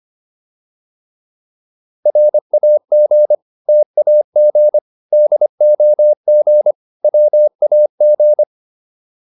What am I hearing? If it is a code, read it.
Morse code, heard as a keyed tone, about 25 wpm, 600 Hz. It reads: RAG TAG DOG WAG
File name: PR302 25